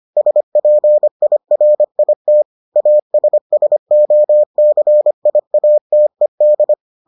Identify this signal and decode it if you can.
Morse code, heard as a keyed tone, about 25 wpm, 600 Hz. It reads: SPIRIT ASSOCIATED